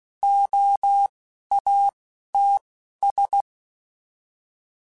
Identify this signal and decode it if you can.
Morse code, heard as a keyed tone, about 16 wpm, spaced as for 11 wpm, 800 Hz. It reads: OATS
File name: Alphabetical / Alphabetical mp3s / Os @10